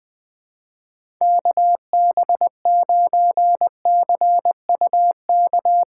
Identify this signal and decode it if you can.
Morse code, heard as a keyed tone, about 20 wpm, 700 Hz. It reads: KB9CUK
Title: USA Call Signs - 6 Characters - 8x Character Spacing - 20wpm